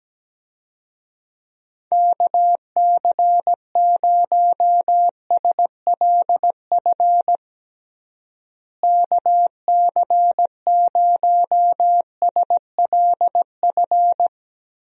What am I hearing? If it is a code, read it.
Morse code, heard as a keyed tone, about 17 wpm, 700 Hz. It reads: KC0SLF KC0SLF
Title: USA Call Signs - 6 Characters - Repeated 3x - 17wpm